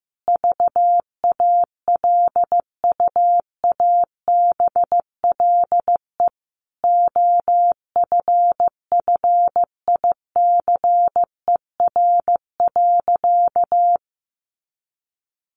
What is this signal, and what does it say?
Morse code, heard as a keyed tone, about 15 wpm, 700 Hz. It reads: VALUABLE OFFICER.
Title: Sentences from Top 1200 Words 15wpm